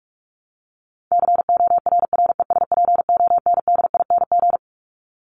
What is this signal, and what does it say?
Morse code, heard as a keyed tone, about 44 wpm, 700 Hz. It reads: CORRESPONDING
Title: ING Suffix - Top 100 Words with Base Words 45wpm